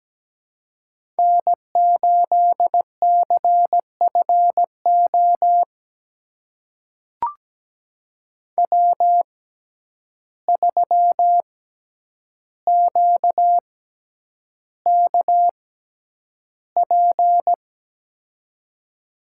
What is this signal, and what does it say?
Morse code, heard as a keyed tone, about 17 wpm, spaced as for 9 wpm, 700 Hz. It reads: N8CFO W3QKP
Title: USA Call Signs - 5 Characters - 6x Character Spacing - 17wpm